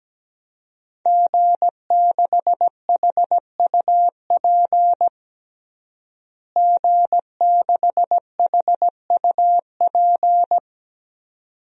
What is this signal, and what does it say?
Morse code, heard as a keyed tone, about 17 wpm, 700 Hz. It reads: G6HUP G6HUP
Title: DXCC-100 Call Signs - 5 Characters - Repeated 2x - 17wpm